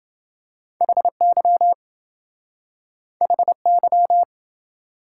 Morse code, heard as a keyed tone, about 27 wpm, 700 Hz. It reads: HY HY